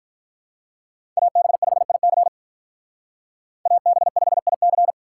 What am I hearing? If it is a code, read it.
Morse code, heard as a keyed tone, about 45 wpm, 700 Hz. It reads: AB5IC AB5IC